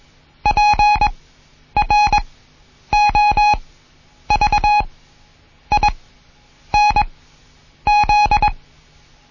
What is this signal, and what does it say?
Morse code, heard as a keyed tone, about 21 wpm, spaced as for 9 wpm, 830 Hz. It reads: PROVINZ
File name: Morsen / Kurs A 20